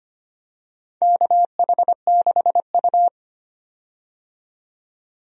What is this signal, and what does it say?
Morse code, heard as a keyed tone, about 25 wpm, 700 Hz. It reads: KH6U
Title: DXCC-100 Call Signs - 4 Characters - 25wpm